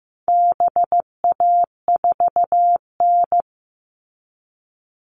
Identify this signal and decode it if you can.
Morse code, heard as a keyed tone, about 15 wpm, 700 Hz. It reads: BA4N
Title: All Cut Numbers - 4-Digit 15wpm